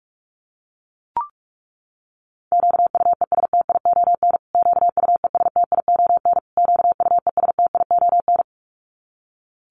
Morse code, heard as a keyed tone, about 44 wpm, 700 Hz. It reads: QUESTION QUESTION QUESTION
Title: Top 500 Words Repeated 45wpm